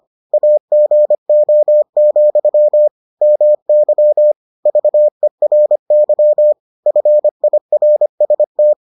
Morse code, heard as a keyed tone, about 25 wpm, 600 Hz. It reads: AGO, MY VERY FIRST